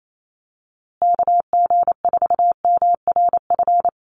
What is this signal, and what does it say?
Morse code, heard as a keyed tone, about 28 wpm, 700 Hz. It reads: KG4MRF
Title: USA Call Signs Repeated 2x 28wpm